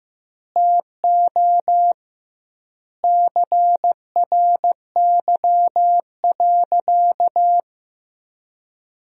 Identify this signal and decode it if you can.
Morse code, heard as a keyed tone, about 16 wpm, 700 Hz. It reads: TO CRY.
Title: 2x Word-Spacing - Sentences from Top 500 Words - 15wpm